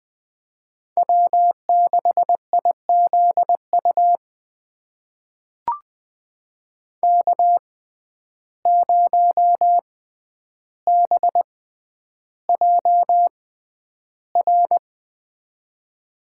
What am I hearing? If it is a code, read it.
Morse code, heard as a keyed tone, about 20 wpm, spaced as for 10 wpm, 700 Hz. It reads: W6IZU K0BJR
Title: USA Call Signs - 6x Character Spacing - 20wpm